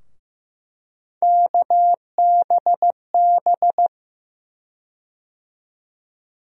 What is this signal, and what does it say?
Morse code, heard as a keyed tone, about 15 wpm, 700 Hz. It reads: KBB
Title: USA Call Sign Postfixes - 3 Letters 15wpm